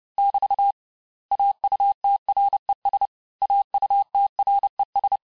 Morse code, heard as a keyed tone, about 30 wpm, 800 Hz. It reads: <BT> AUTRES AUTRES
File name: cours-du-mercredi / audios / cours-854-mots-vitesse-28-mpm